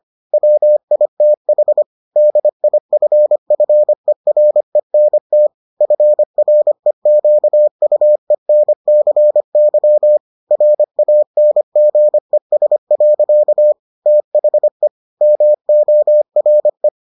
Morse code, heard as a keyed tone, about 25 wpm, 600 Hz. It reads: WITH DIFFERENT FREQUENCY RANGES. THE MORE